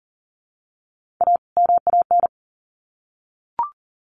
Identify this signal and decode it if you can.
Morse code, heard as a keyed tone, about 40 wpm, 700 Hz. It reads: A MAN
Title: Sets of 2 Words 40wpm